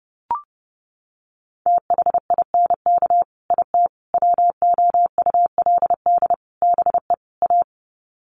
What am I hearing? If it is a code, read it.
Morse code, heard as a keyed tone, about 30 wpm, 700 Hz. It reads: THINK IT WOULD BE A
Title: Sets of 5 Words Encore2 30wpm